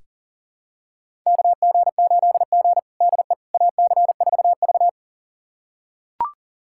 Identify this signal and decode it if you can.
Morse code, heard as a keyed tone, about 40 wpm, 700 Hz. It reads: KG8G DE AC4V
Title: USA Callsign DE Callsign - 4 Characters - 40wpm